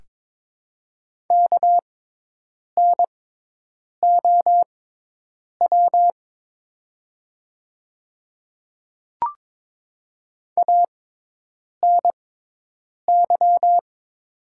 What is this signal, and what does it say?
Morse code, heard as a keyed tone, about 22 wpm, spaced as for 8 wpm, 700 Hz. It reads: KNOW ANY
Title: 6x Character Spacing - Top 200 Words - 22wpm